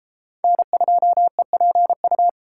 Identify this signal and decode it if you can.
Morse code, heard as a keyed tone, about 29 wpm, 700 Hz. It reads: N2EPU